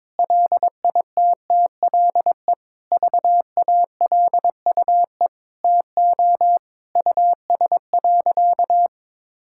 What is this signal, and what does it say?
Morse code, heard as a keyed tone, about 22 wpm, 700 Hz. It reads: LITTLE VALUE TO US.
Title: Sentences from Top 1000 Words - Review Part 5 - 22wpm